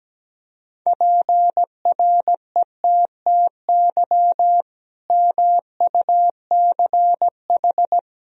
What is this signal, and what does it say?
Morse code, heard as a keyed tone, about 17 wpm, 700 Hz. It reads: PRETTY MUCH
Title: Sentences from Top 1000 Words - Review Part 2 - 17wpm